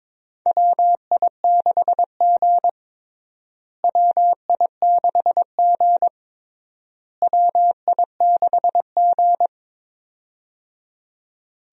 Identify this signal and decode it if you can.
Morse code, heard as a keyed tone, about 22 wpm, 700 Hz. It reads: WI6G WI6G WI6G
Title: USA Call Signs - 4 Characters - Repeated 3x - 22wpm